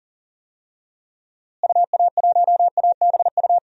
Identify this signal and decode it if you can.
Morse code, heard as a keyed tone, about 40 wpm, 700 Hz. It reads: UA1ABU
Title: DXCC-100 Call Signs - 6 Characters - Repeated 2x - 40wpm